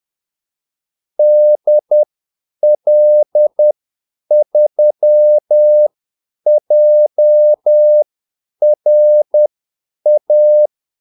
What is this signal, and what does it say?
Morse code, heard as a keyed tone, about 10 wpm, spaced as for 8 wpm, 600 Hz. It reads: DL3JRA
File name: Callsigns-10-8wpm